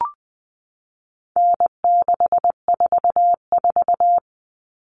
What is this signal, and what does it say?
Morse code, heard as a keyed tone, about 20 wpm, 700 Hz. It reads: N644
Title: All Cut Numbers - 4-Digit 20wpm